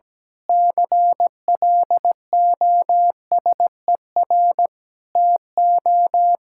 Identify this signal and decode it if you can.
Morse code, heard as a keyed tone, about 17 wpm, 700 Hz. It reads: CLOSER TO
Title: Pangram Challenge v2 - 17wpm